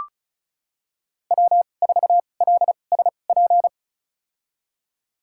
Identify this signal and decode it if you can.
Morse code, heard as a keyed tone, about 36 wpm, spaced as for 25 wpm, 700 Hz. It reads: W4LSP
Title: USA Call Signs - 2x Character Spacing - 35wpm